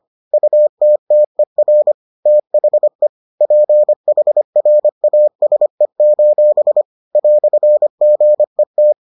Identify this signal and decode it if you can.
Morse code, heard as a keyed tone, about 25 wpm, 600 Hz. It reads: UTTER THE PHRASE: "GET